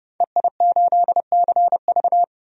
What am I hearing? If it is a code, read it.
Morse code, heard as a keyed tone, about 30 wpm, 700 Hz. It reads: EI8CV
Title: DXCC-100 Call Signs - 5 Characters - 30wpm